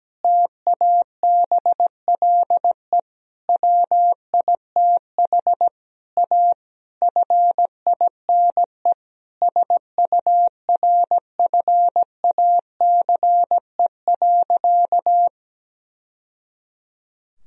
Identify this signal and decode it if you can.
Morse code, heard as a keyed tone, about 17 wpm, 700 Hz. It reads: TABLE WITH A FINE SURFACE.
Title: Sentences from Top 1000 Words - Review Part 3 - 17wpm